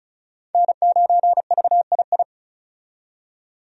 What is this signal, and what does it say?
Morse code, heard as a keyed tone, about 36 wpm, 700 Hz. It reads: N9VII
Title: USA Call Signs - 5 Characters - 35wpm